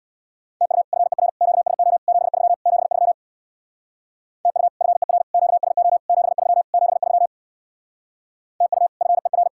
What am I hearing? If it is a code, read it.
Morse code, heard as a keyed tone, about 61 wpm, 700 Hz. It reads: TU FER QSO 73 73 TU FER QSO 73 73 TU FER
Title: Speed-Racing Common QSO Phrases - 50wpm